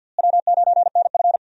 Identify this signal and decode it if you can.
Morse code, heard as a keyed tone, about 51 wpm, 700 Hz. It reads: W9NF
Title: USA Call Signs - 4 Characters - Repeated 2x - 50wpm